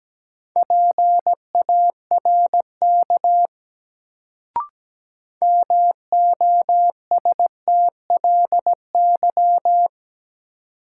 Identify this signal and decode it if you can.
Morse code, heard as a keyed tone, about 17 wpm, 700 Hz. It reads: PARK MOSTLY
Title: Sentences from Top 1300 Words - New Word Drill - 17wpm